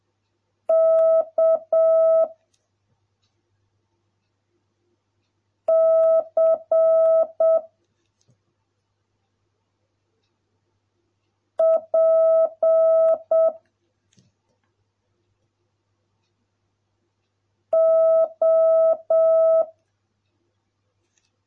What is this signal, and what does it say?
Morse code, heard as a keyed tone, about 7 wpm, 650 Hz. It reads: K C P O